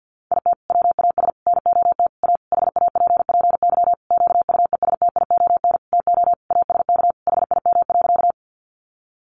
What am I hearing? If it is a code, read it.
Morse code, heard as a keyed tone, about 50 wpm, 700 Hz. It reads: IT WAS NOT A HAPPY QUESTION TO ASK HIM.
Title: Sentences from Top 1000 Words 50wpm